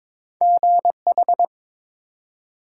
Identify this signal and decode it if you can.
Morse code, heard as a keyed tone, about 22 wpm, 700 Hz. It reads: GH
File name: practice / 22wpm / Two Letter-Number Combinations - 22wpm